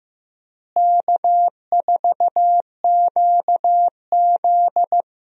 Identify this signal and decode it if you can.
Morse code, heard as a keyed tone, about 15 wpm, 700 Hz. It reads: K4QZ
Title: USA Call Signs - 4 Characters - Repeated 2x - 15wpm